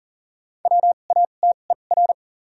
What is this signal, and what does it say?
Morse code, heard as a keyed tone, about 40 wpm, spaced as for 29 wpm, 700 Hz. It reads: WATER
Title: 2x Character Spacing - Top 100 Words - 40wpm